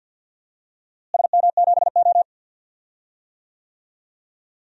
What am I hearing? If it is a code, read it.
Morse code, heard as a keyed tone, about 53 wpm, 700 Hz. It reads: SM7O
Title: DXCC-100 Call Signs - 4 Characters - Repeated 3x - 50wpm